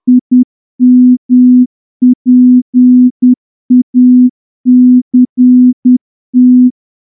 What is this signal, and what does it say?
Morse code, heard as a keyed tone, about 10 wpm, 250 Hz. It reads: IMPACT
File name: morse-training-10WPM